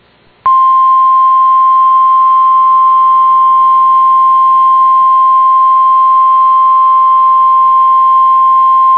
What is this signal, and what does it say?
Emergency Alert System (SAME) tones (the 1050 Hz attention tone). It is heard from 0.6 to 8.3 s.